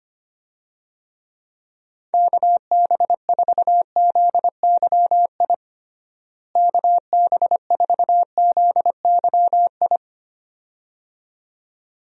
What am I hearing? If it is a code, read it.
Morse code, heard as a keyed tone, about 25 wpm, 700 Hz. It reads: KB4ZYI KB4ZYI
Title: USA Call Signs - 6 Characters - Repeated 2x - 25wpm